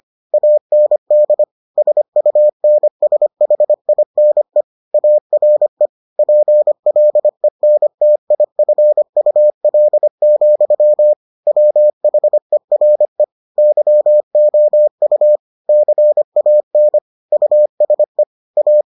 Morse code, heard as a keyed tone, about 25 wpm, 600 Hz. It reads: AND SUNSHINE ARE PLENTIFUL, WHERE YOU CAN USE A